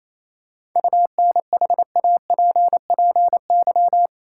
Morse code, heard as a keyed tone, about 28 wpm, 700 Hz. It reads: UNHAPPY
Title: Seven Letter Words - 28wpm